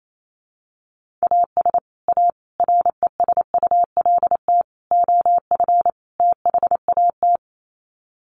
Morse code, heard as a keyed tone, about 28 wpm, 700 Hz. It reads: AS A RESULT OF THAT
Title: Sets of 5 Words Encore 28wpm